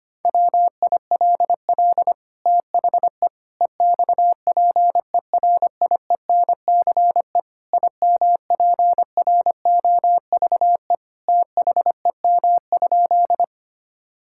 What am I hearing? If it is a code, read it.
Morse code, heard as a keyed tone, about 25 wpm, 700 Hz. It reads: WILL THE EXPERIENCE IMPROVE THEM?
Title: Sentences from Top 1000 Words - Review Part 5 - 25wpm